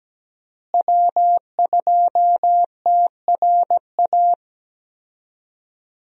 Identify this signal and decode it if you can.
Morse code, heard as a keyed tone, about 17 wpm, 700 Hz. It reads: W2TRA
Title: USA Call Signs - 5 Characters - Repeated 3x - 17wpm